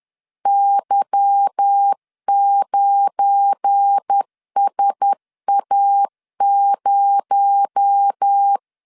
Morse code, heard as a keyed tone, about 10 wpm, 800 Hz. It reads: Y9SA0
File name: LEZIONI / 10 SIGNS / 10C DOT 10wpm 2940